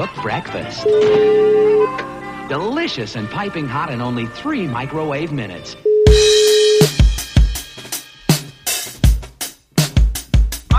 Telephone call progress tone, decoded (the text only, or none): ringback tone